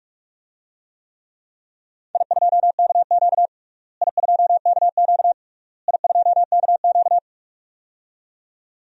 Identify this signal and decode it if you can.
Morse code, heard as a keyed tone, about 42 wpm, 700 Hz. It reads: I2KQ I2KQ I2KQ
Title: DXCC-100 Call Signs - 4 Characters - Repeated 3x - 45wpm